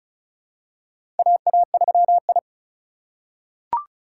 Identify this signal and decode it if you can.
Morse code, heard as a keyed tone, about 36 wpm, 700 Hz. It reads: AA3I
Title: USA Call Signs - 4 Characters - 4x Character Spacing - 35wpm